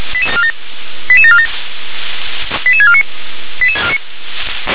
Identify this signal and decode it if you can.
Five-tone selective call (ZVEI), heard as quick runs of five tones, tones of about 70 ms. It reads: 88747 88649